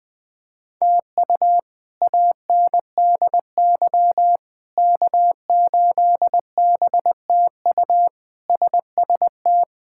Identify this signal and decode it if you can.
Morse code, heard as a keyed tone, about 20 wpm, 700 Hz. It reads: TU ANDY K8BTU SST
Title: K1USN SST QSO Practice - 20wpm